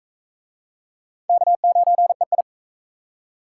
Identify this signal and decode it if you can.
Morse code, heard as a keyed tone, about 40 wpm, 700 Hz. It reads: K9EI